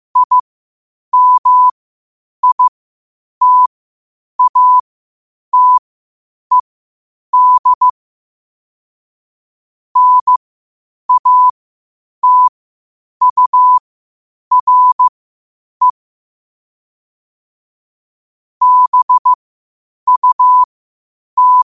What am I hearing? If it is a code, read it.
Morse code, heard as a keyed tone, about 15 wpm, spaced as for 8 wpm, 1000 Hz. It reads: IMITATED NATURE BUT
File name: qotdfiles / 8wpm